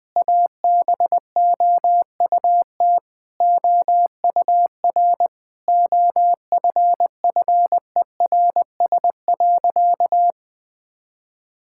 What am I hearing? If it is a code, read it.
Morse code, heard as a keyed tone, about 20 wpm, 700 Hz. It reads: ABOUT OUR OFFERS.
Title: Sentences with 6-Word Phrases - 20wpm